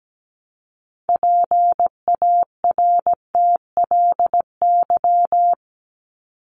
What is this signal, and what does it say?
Morse code, heard as a keyed tone, about 17 wpm, 700 Hz. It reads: PARTLY
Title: LY Suffix - Top 100 Words with Base Words 17wpm